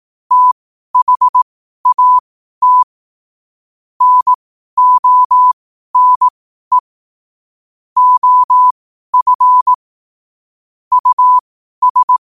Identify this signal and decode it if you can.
Morse code, heard as a keyed tone, about 18 wpm, spaced as for 12 wpm, 1000 Hz. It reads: THAT NONE OF US
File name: qotdfiles / 12wpm